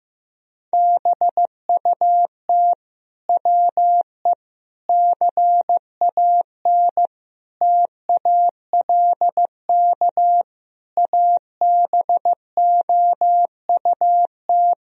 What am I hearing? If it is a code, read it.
Morse code, heard as a keyed tone, about 15 wpm, 700 Hz. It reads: BUT WE CAN TALK ABOUT
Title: Sentences from Top 1000 Words - Review Part 4 - 15wpm